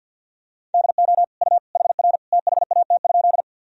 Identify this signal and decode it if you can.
Morse code, heard as a keyed tone, about 50 wpm, 700 Hz. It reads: DO U HR THAT?
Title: Sentences from Top 300 Words and Abbreviations 50wpm